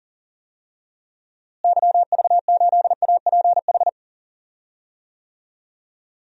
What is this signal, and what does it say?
Morse code, heard as a keyed tone, about 40 wpm, 700 Hz. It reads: YV8APH
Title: DXCC-100 Call Signs - 6 Characters - Repeated 3x - 40wpm